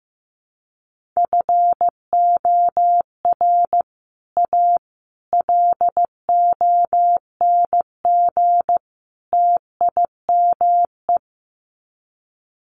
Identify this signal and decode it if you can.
Morse code, heard as a keyed tone, about 15 wpm, 700 Hz. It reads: FOR A LONG TIME
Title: Sets of 4 Words 15wpm